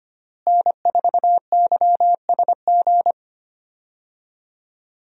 Morse code, heard as a keyed tone, about 25 wpm, 700 Hz. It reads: N4YSG
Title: USA Call Signs - 5 Characters - Repeated 3x - 25wpm